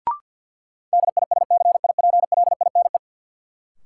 Morse code, heard as a keyed tone, about 53 wpm, 700 Hz. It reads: DISCIPLINE
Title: Sentences from Top 900 Words - New Word Drill - 50wpm